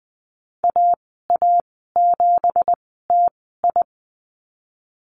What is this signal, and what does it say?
Morse code, heard as a keyed tone, about 20 wpm, spaced as for 15 wpm, 700 Hz. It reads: AA7TI
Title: 193 Lesson 31g - Letter v - Call Signs 20wpm